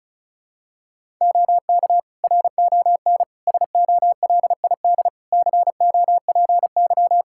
Morse code, heard as a keyed tone, about 36 wpm, 700 Hz. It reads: OK RON SOLID COPY
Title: Practice QSO - Exchange 3 - More Info - 35wpm